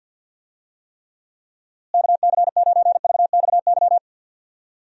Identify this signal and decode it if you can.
Morse code, heard as a keyed tone, about 53 wpm, 700 Hz. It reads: KC9VXY
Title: USA Call Signs - 6 Characters - Repeated 2x - 50wpm